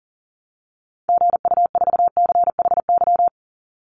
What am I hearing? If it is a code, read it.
Morse code, heard as a keyed tone, about 40 wpm, 700 Hz. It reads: GU4CHY
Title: Call Signs - 40wpm